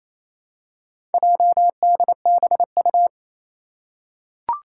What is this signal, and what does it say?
Morse code, heard as a keyed tone, about 28 wpm, 700 Hz. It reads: JDBU
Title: Four Challenging Letter-Number Combinations 28wpm